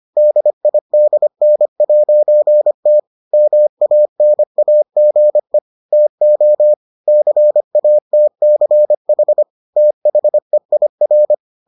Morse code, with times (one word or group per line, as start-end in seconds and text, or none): 0.2-3.0 DIDN'T
3.3-5.6 MANAGE
5.9-6.7 TO
7.1-9.4 CATCH
9.8-11.3 THEIR